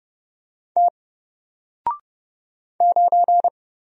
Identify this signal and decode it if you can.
Morse code, heard as a keyed tone, about 30 wpm, 700 Hz. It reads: T 9
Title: Single Letter-Number Alt 30wpm